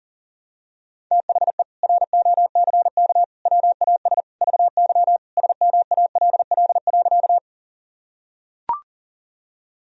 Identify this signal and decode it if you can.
Morse code, heard as a keyed tone, about 40 wpm, 700 Hz. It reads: THE ROCK WAS VY SMALL.
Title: Sentences from Top 400 Words and Abbreviations 40wpm